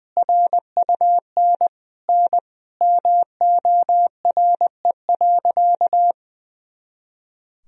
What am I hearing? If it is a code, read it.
Morse code, heard as a keyed tone, about 21 wpm, 700 Hz. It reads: RUN N MORE.
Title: Sentences from Top 300 Words and Abbreviations 20wpm